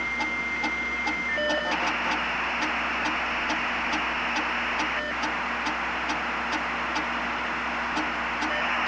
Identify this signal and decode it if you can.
Dial-up modem handshake.
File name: costumsoundshlrisen / computer dialup